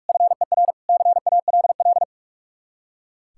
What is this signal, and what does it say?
Morse code, heard as a keyed tone, about 42 wpm, 700 Hz. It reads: FER CALL
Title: K1USN SST QSO Practice 45wpm